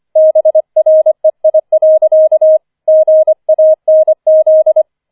Morse code, heard as a keyed tone, about 25 wpm, 630 Hz. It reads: BREI. GANZ
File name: 25wpm